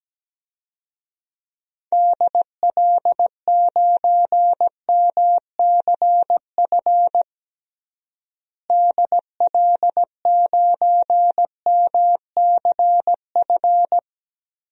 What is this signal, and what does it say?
Morse code, heard as a keyed tone, about 17 wpm, 700 Hz. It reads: DL9MCF DL9MCF
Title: DXCC-100 Call Signs - 6 Characters - Repeated 3x - 17wpm